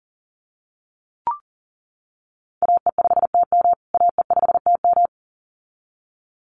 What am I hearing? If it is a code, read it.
Morse code, heard as a keyed tone, about 40 wpm, 700 Hz. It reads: AE5TM AE5TM
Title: USA Call Signs Repeated 2x 40wpm